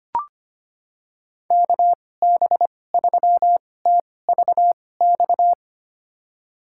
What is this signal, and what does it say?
Morse code, heard as a keyed tone, about 25 wpm, spaced as for 18 wpm, 700 Hz. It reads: KB3TVX
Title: USA Call Signs - 6 Characters - 2x Character Spacing - 25wpm